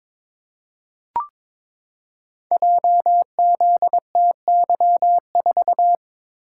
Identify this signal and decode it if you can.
Morse code, heard as a keyed tone, about 22 wpm, 700 Hz. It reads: JZTY4
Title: Code Groups - 5 Letter:Numbers - 22wpm